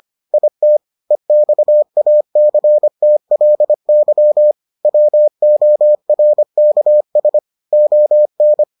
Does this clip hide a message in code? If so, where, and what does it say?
Yes: Morse code from 0.3 to 8.6 s, heard as a keyed tone, about 25 wpm, 600 Hz. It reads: IT EXACTLY WORKS ON